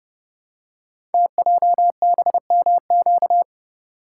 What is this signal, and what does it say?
Morse code, heard as a keyed tone, about 29 wpm, 700 Hz. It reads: TJBMQ
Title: Code Groups - 5 Letters - 30wpm